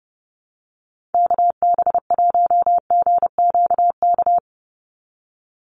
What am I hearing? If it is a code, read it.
Morse code, heard as a keyed tone, about 30 wpm, 700 Hz. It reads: KB1GQK
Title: USA Call Signs 30wpm